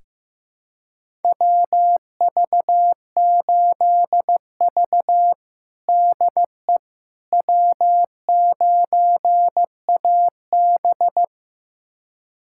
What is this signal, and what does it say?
Morse code, heard as a keyed tone, about 15 wpm, 700 Hz. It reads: WV8V DE W9AB
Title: USA Callsign DE Callsign - 4 Characters - 15wpm